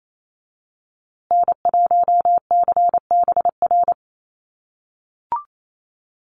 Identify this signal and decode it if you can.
Morse code, heard as a keyed tone, about 28 wpm, 700 Hz. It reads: N1CBR
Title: USA Call Signs Repeated 3x 28wpm